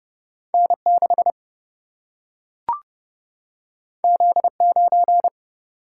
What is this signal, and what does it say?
Morse code, heard as a keyed tone, about 30 wpm, 700 Hz. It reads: N6 Z9
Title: Single-Letter Single-Number 30wpm